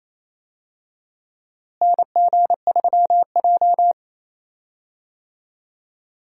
Morse code, heard as a keyed tone, about 28 wpm, 700 Hz. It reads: NG3J
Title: USA Call Signs - 4 Characters - Repeated 2x - 28wpm